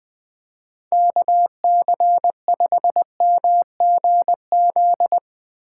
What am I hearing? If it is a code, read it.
Morse code, heard as a keyed tone, about 20 wpm, 700 Hz. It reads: KC5MGZ
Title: Speed Racing - USA 2x3 Call Signs 17wpm